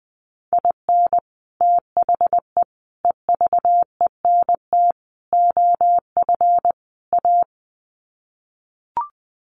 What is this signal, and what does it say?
Morse code, heard as a keyed tone, about 20 wpm, 700 Hz. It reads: IN THE EVENT OF A
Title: Sets of 5 Words Encore 20wpm